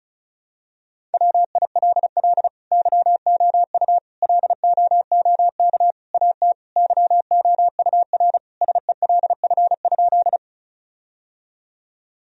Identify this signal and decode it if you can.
Morse code, heard as a keyed tone, about 34 wpm, 700 Hz. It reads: WILL YOU LOOK AT YOURSELF?
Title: Sentences from Top 1000 Words - Review Part 1 - 35wpm